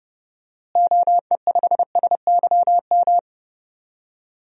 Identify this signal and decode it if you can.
Morse code, heard as a keyed tone, about 32 wpm, 700 Hz. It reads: OE5SYM
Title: DXCC-100 Call Signs - 6 Characters - Repeated 2x - 30wpm